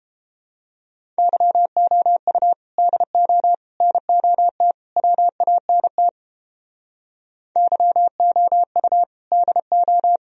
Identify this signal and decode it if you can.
Morse code, heard as a keyed tone, about 29 wpm, 700 Hz. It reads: YOU DO NOT WANT YOU DO